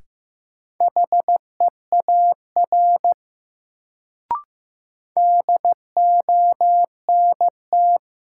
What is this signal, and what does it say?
Morse code, heard as a keyed tone, about 15 wpm, 700 Hz. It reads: HEAR DONT
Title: Sentences from Top 300 Words - New Word Drill - 15wpm